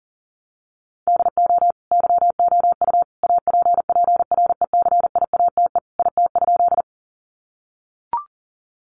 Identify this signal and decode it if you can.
Morse code, heard as a keyed tone, about 40 wpm, 700 Hz. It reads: DO YOU APPRECIATE IT?
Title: Sentences from Top 1300 Words 40wpm